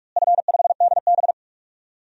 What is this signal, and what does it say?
Morse code, heard as a keyed tone, about 47 wpm, 700 Hz. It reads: F5DB
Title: DXCC-100 Call Signs - 4 Characters - Repeated 2x - 45wpm